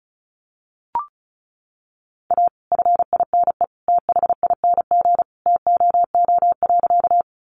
Morse code, heard as a keyed tone, about 34 wpm, 700 Hz. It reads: A FINE THING TOO.
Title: Sentences from Top 500 Words 35wpm